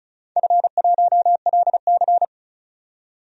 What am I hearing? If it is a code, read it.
Morse code, heard as a keyed tone, about 34 wpm, 700 Hz. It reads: F1LC
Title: DXCC-100 Call Signs - 4 Characters - 35wpm